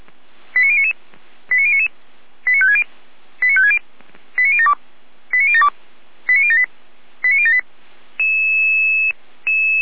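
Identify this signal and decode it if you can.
Five-tone selective call (ZVEI), heard as quick runs of five tones, tones of about 70 ms. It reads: 89099 89099 89560 89560 89072 89072 89078 89078
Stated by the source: Five-tone ZVEI selective call 89150, received on SDR.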